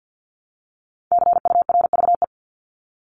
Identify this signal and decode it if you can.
Morse code, heard as a keyed tone, about 50 wpm, 700 Hz. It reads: CURVE
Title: Top 1501-1750 Words 50wpm